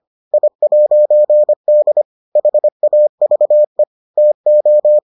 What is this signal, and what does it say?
Morse code, heard as a keyed tone, about 25 wpm, 600 Hz. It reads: I'D HAVE TO